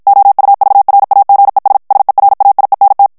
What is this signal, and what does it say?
Morse code, heard as a keyed tone, about 56 wpm, 800 Hz. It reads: OUVRAGES SERAIENT
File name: FRANCAIS TEXTE CLAIR 2 VITESSE 50 MOTS MINUTES